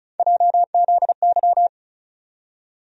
Morse code, heard as a keyed tone, about 36 wpm, 700 Hz. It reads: JZY
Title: USA Call Sign Postfixes - 3 Letters 35wpm